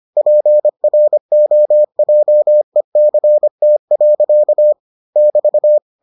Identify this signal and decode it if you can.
Morse code, heard as a keyed tone, about 25 wpm, 600 Hz. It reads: PROJECT. <BT>